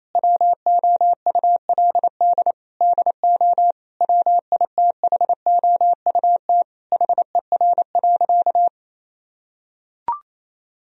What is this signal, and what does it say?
Morse code, heard as a keyed tone, about 27 wpm, 700 Hz. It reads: WOULD DO WITHOUT HER.
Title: Sentences from Top 1000 Words - Review Part 3 - 28wpm